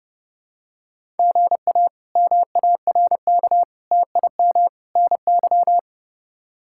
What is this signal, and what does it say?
Morse code, heard as a keyed tone, about 32 wpm, 700 Hz. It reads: GA MARK TIM NY
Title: K1USN SST QSO Practice - 30wpm